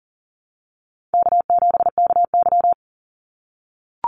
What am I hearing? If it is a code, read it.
Morse code, heard as a keyed tone, about 41 wpm, 700 Hz. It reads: K7KY
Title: Call Signs - 40wpm